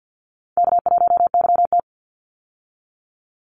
Morse code, heard as a keyed tone, about 50 wpm, 700 Hz. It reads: K1YT
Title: Call Signs Repeated 3x - 50wpm